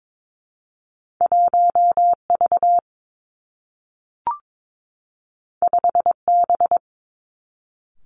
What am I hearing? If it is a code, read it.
Morse code, heard as a keyed tone, about 22 wpm, 700 Hz. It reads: 1 V 5 B
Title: Two Challenging Letter-Number Combinations 22wpm